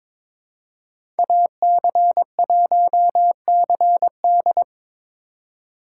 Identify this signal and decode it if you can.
Morse code, heard as a keyed tone, about 22 wpm, 700 Hz. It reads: AC1CD
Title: USA Call Signs - 5 Characters - 4x Character Spacing - 22wpm